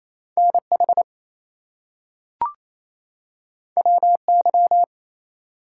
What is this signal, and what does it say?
Morse code, heard as a keyed tone, about 28 wpm, 700 Hz. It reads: NH WY
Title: US States and Canadian Provinces 28wpm